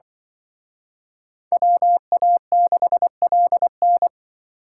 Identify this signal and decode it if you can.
Morse code, heard as a keyed tone, about 24 wpm, 700 Hz. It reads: WA6LN
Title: Speed Racing - USA Call Signs 20wpm